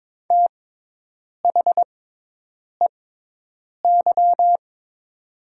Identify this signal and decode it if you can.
Morse code, heard as a keyed tone, about 22 wpm, spaced as for 8 wpm, 700 Hz. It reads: THEY
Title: 6x Character Spacing - Top 200 Words - 22wpm